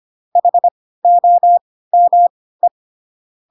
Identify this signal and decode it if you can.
Morse code, heard as a keyed tone, about 25 wpm, spaced as for 16 wpm, 700 Hz. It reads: HOME